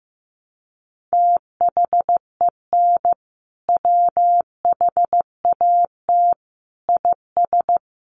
Morse code, heard as a keyed tone, about 15 wpm, 700 Hz. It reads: THEN WHAT IS
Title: Sentences from Top 100 Words 15wpm